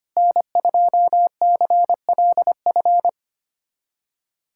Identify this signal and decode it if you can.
Morse code, heard as a keyed tone, about 25 wpm, 700 Hz. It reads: N2CLF
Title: USA Callsign DE Callsign - 5 Characters - 25wpm